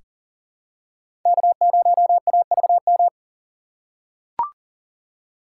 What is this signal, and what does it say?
Morse code, heard as a keyed tone, about 40 wpm, 700 Hz. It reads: K0AVM
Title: USA Call Signs - 5 Characters - 4x Character Spacing - 40wpm